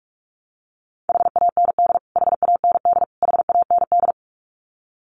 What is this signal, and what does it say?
Morse code, heard as a keyed tone, about 44 wpm, 700 Hz. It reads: HAND HAND HAND
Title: Top 500 Words Repeated 45wpm